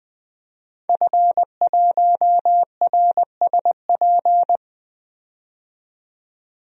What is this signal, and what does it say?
Morse code, heard as a keyed tone, about 20 wpm, 700 Hz. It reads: F1RSP
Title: DXCC-100 Call Signs - 5 Characters - 20wpm